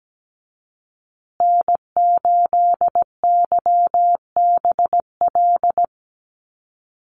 Morse code, heard as a keyed tone, about 17 wpm, 700 Hz. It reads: N8YBL
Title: USA Call Signs Repeated 3x 17wpm